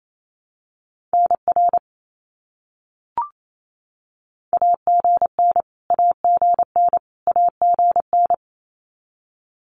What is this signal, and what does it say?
Morse code, heard as a keyed tone, about 28 wpm, 700 Hz. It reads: NR AGN AGN AGN
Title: Abbreviations Repeated 3x - 28wpm